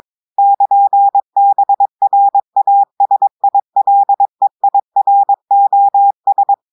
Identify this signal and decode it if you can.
Morse code, heard as a keyed tone, about 22 wpm, 800 Hz. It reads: <KN>BRASILEIROS
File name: audios / portugais / portugais texte en clair 1 vitesse 22 mots minutes